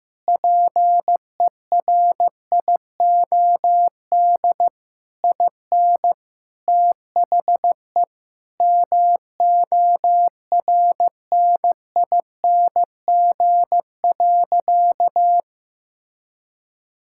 Morse code, heard as a keyed tone, about 15 wpm, 700 Hz. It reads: PERIOD IN THE MORNING.
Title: Sentences from Top 1000 Words - Review Part 5 - 15wpm